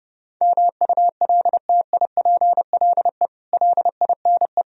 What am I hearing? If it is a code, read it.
Morse code, heard as a keyed tone, about 31 wpm, 700 Hz. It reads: MULTIPLE LINE